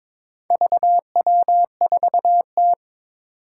Morse code, heard as a keyed tone, about 22 wpm, 700 Hz. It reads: VW4T
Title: Four Challenging Letter-Number Combinations 22wpm